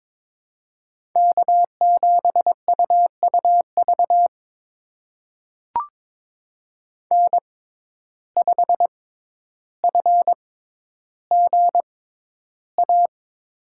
Morse code, heard as a keyed tone, about 22 wpm, spaced as for 11 wpm, 700 Hz. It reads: K7UUV N5FGA
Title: USA Call Signs - 5 Characters - 6x Character Spacing - 22wpm